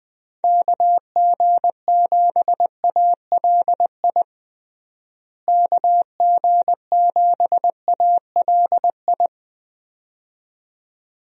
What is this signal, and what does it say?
Morse code, heard as a keyed tone, about 20 wpm, 700 Hz. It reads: KG7ALI KG7ALI
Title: USA Call Signs - 6 Characters - Repeated 2x - 20wpm